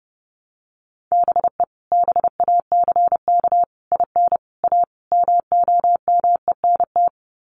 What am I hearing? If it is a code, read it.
Morse code, heard as a keyed tone, about 30 wpm, 700 Hz. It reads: BE BACK IN A MOMENT
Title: Sets of 5 Words 30wpm